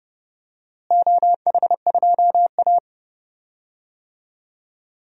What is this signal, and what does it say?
Morse code, heard as a keyed tone, about 30 wpm, 700 Hz. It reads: OH2A
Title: DXCC-100 Call Signs - 4 Characters - Repeated 3x - 30wpm